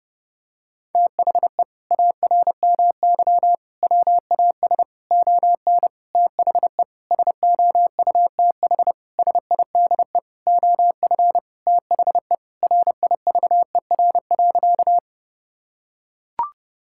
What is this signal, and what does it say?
Morse code, heard as a keyed tone, about 30 wpm, 700 Hz. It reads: THE ARMY WAS ON THE SOUTH SIDE OF THE RIVER.
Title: Sentences from Top 1000 Words - Review Part 5 - 30wpm